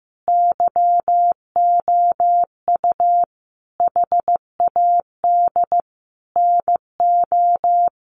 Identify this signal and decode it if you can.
Morse code, heard as a keyed tone, about 15 wpm, 700 Hz. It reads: YOU HAD NO
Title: Sentences from Top 1400 Words 15wpm